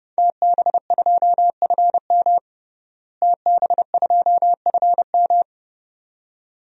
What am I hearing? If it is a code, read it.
Morse code, heard as a keyed tone, about 30 wpm, 700 Hz. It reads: TB2FM TB2FM
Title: DXCC-100 Call Signs - 5 Characters - Repeated 2x - 30wpm